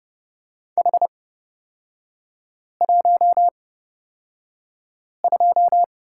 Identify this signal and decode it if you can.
Morse code, heard as a keyed tone, about 31 wpm, spaced as for 5 wpm, 690 Hz. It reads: H12
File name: Lesson 38e - Letter q - Three Characters 30wpm